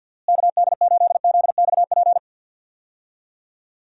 Morse code, heard as a keyed tone, about 46 wpm, 700 Hz. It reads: KD8ZXP
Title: USA Call Signs - 6 Characters - Repeated 3x - 50wpm